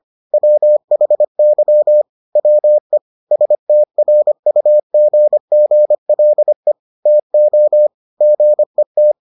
Morse code, heard as a keyed tone, about 25 wpm, 600 Hz. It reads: WHY WE STRUGGLE TO GET